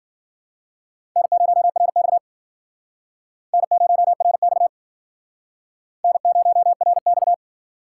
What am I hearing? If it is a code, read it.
Morse code, heard as a keyed tone, about 51 wpm, 700 Hz. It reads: N0RX N0RX N0RX